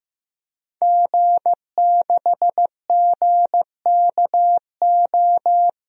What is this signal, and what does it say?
Morse code, heard as a keyed tone, about 15 wpm, 700 Hz. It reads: G6GKO
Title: DXCC-100 Call Signs - 5 Characters - Repeated 3x - 15wpm